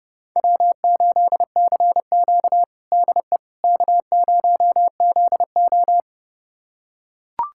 Morse code, heard as a keyed tone, about 30 wpm, 700 Hz. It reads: W8CQ DE K0ZO
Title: USA Callsign DE Callsign - 4 Characters - 30wpm